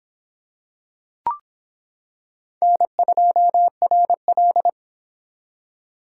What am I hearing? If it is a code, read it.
Morse code, heard as a keyed tone, about 25 wpm, 700 Hz. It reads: N2RL